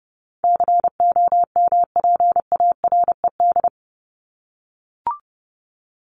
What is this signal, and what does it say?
Morse code, heard as a keyed tone, about 30 wpm, 700 Hz. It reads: COMPARED
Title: ED Suffix - Top 100 Words with Base Words 30wpm